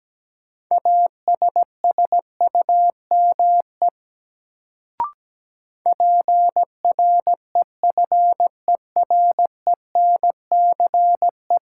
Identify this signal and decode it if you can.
Morse code, heard as a keyed tone, about 17 wpm, 700 Hz. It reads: ASSUME PREFERENCE
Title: Sentences from Top 1500 Words - New Word Drill - 17wpm